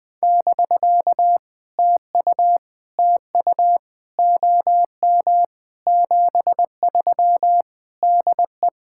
Morse code, heard as a keyed tone, about 20 wpm, 700 Hz. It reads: <BK> TU TU OM 73 DE
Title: POTA QSO Practice - 20wpm